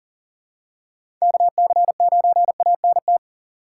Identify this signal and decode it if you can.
Morse code, heard as a keyed tone, about 40 wpm, 700 Hz. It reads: KC9ANT
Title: USA Call Signs - 6 Characters - 3x Character Spacing - 40wpm